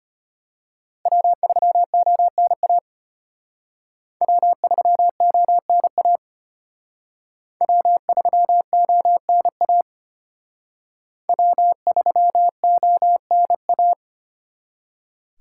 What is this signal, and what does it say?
Morse code, heard as a keyed tone, about 31 wpm, 700 Hz. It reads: W3ONA W3ONA W3ONA W3ONA
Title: Speed Racing - USA 1x3 Call Signs 25wpm